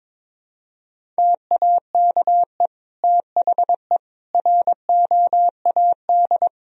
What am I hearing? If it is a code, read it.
Morse code, heard as a keyed tone, about 21 wpm, 700 Hz. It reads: TAKE THE ROAD
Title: Sentences from Top 400 Words and Abbreviations 22wpm